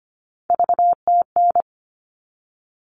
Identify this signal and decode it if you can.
Morse code, heard as a keyed tone, about 25 wpm, 700 Hz. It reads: VTN
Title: All Cut Numbers - 3-Digit 25wpm